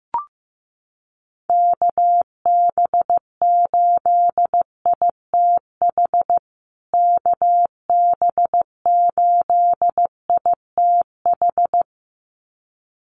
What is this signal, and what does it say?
Morse code, heard as a keyed tone, about 15 wpm, 700 Hz. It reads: KB8ITH KB8ITH
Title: USA Call Signs Repeated 2x 15wpm